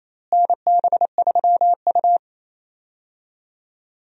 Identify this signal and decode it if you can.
Morse code, heard as a keyed tone, about 28 wpm, 700 Hz. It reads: NB3U
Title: USA Call Signs - 4 Characters - Repeated 2x - 28wpm